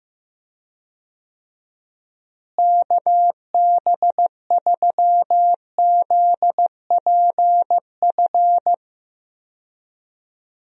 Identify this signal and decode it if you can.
Morse code, heard as a keyed tone, about 15 wpm, 700 Hz. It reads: KB3ZPF
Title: USA Call Signs - 6 Characters - Repeated 3x - 15wpm